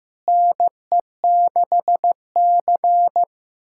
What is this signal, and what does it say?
Morse code, heard as a keyed tone, about 15 wpm, 700 Hz. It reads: NE6C
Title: USA Callsign DE Callsign - 4 Characters - 15wpm